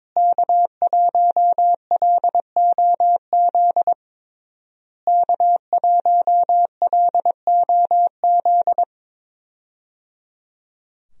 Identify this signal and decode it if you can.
Morse code, heard as a keyed tone, about 21 wpm, 700 Hz. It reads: K1LOZ K1LOZ
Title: USA Call Signs - 5 Characters - Repeated 3x - 22wpm